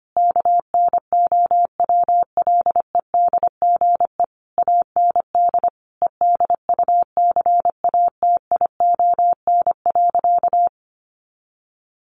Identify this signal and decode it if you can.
Morse code, heard as a keyed tone, about 25 wpm, 700 Hz. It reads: KNOWLEDGE AND EDUCATION.
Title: Sentences from Top 1500 Words 25wpm